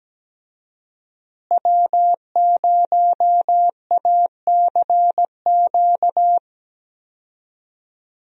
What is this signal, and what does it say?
Morse code, heard as a keyed tone, about 17 wpm, 700 Hz. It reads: W0ACQ
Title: USA Call Signs - 5 Characters - Repeated 3x - 17wpm